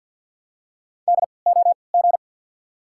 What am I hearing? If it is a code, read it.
Morse code, heard as a keyed tone, about 50 wpm, spaced as for 28 wpm, 690 Hz. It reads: DOG
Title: 3x Character Spacing - Top 500 Words - 50wpm